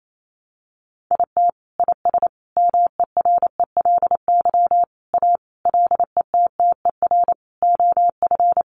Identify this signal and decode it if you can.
Morse code, heard as a keyed tone, about 28 wpm, 700 Hz. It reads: IT IS MERELY A LETTER OF
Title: Sentences from Top 1400 Words 28wpm